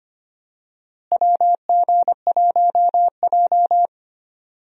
Morse code, heard as a keyed tone, about 25 wpm, 700 Hz. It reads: WG1J
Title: USA Call Signs - 4 Characters - Repeated 3x - 25wpm